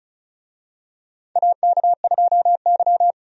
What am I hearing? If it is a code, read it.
Morse code, heard as a keyed tone, about 36 wpm, 700 Hz. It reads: AK2Y
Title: USA Call Signs - 4 Characters - 35wpm